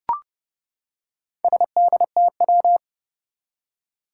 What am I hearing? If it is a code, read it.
Morse code, heard as a keyed tone, about 30 wpm, 700 Hz. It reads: SDTW
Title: Four Challenging Letter-Number Combinations 30wpm